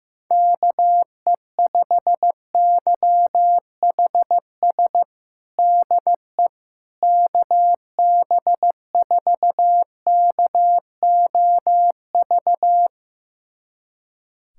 Morse code, heard as a keyed tone, about 15 wpm, 700 Hz. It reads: KE5YHS DE KB4KOV
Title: USA Callsign DE Callsign - 6 Characters - 15wpm